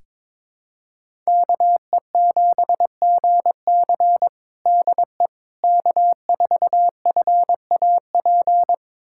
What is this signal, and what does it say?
Morse code, heard as a keyed tone, about 22 wpm, 700 Hz. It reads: KE7GC DE K4FAP
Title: USA Callsign DE Callsign 22wpm